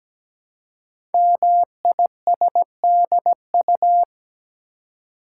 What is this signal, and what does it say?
Morse code, heard as a keyed tone, about 17 wpm, 700 Hz. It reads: MISDU
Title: Code Groups - 5 Letters - 17wpm